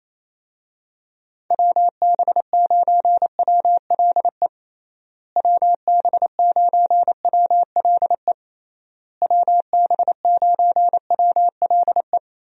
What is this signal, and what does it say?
Morse code, heard as a keyed tone, about 27 wpm, 700 Hz. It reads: WB9WLE WB9WLE WB9WLE
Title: USA Call Signs - 6 Characters - Repeated 3x - 28wpm